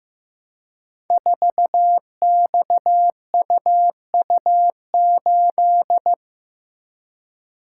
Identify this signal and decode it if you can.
Morse code, heard as a keyed tone, about 15 wpm, 700 Hz. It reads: 4XUU8
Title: Code Groups - 5 Letter:Numbers - 15wpm